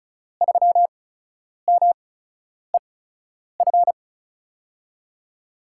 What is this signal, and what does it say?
Morse code, heard as a keyed tone, about 36 wpm, spaced as for 10 wpm, 690 Hz. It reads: 3MEF
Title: USA Call Signs - 5 Characters - 8x Character Spacing - 35wpm